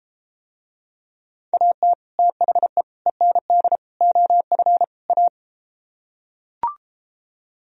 Morse code, heard as a keyed tone, about 33 wpm, 700 Hz. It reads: AT THE END OF A